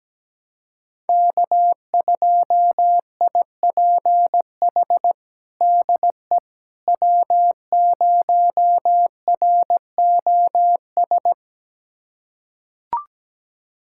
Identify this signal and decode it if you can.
Morse code, heard as a keyed tone, about 17 wpm, 700 Hz. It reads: K2IPH DE W0ROS
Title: USA Callsign DE Callsign - 5 Characters - 17wpm